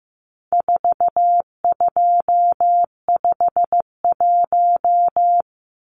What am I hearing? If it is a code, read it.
Morse code, heard as a keyed tone, about 15 wpm, 700 Hz. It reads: 4251
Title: Numbers - 4-Digit 15wpm